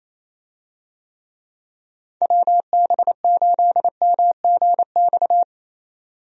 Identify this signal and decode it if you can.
Morse code, heard as a keyed tone, about 27 wpm, 700 Hz. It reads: WB8MGX
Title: USA Call Signs - 6 Characters - 28wpm